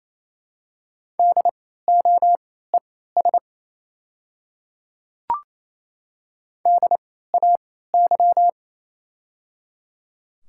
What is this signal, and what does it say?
Morse code, heard as a keyed tone, about 28 wpm, spaced as for 16 wpm, 700 Hz. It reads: DOES DAY
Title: 3x Character Spacing - Top 300 Words - 28wpm